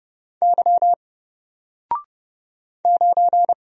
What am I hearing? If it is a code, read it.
Morse code, heard as a keyed tone, about 30 wpm, 700 Hz. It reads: Y 9
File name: Single-Letter-Number-30wpm